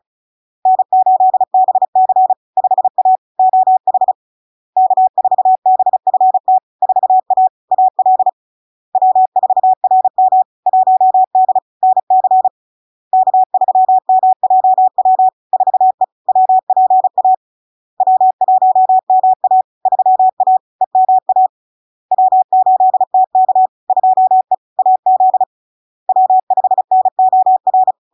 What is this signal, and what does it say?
Morse code, heard as a keyed tone, about 36 wpm, 750 Hz. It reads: N8BC 5A OH K4BFT 4A AL W4RM 1D NC K3MJW 4E WPA W1MA 3A EMA W8TK 2E AZ W5NOR
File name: LARK-FD-2022-Log-in-CW-35wpm-750-Hz